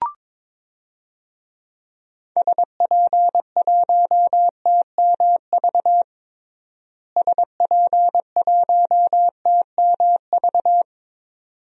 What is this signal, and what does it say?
Morse code, heard as a keyed tone, about 22 wpm, 700 Hz. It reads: SP1TMV SP1TMV
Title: DXCC-100 Call Signs - 6 Characters - Repeated 2x - 22wpm